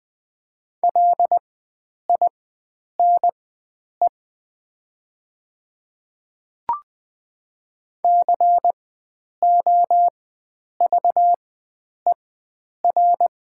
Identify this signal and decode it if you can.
Morse code, heard as a keyed tone, about 20 wpm, spaced as for 9 wpm, 700 Hz. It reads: LINE COVER
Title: 4x Character Spacing - Top 300 Words - 20wpm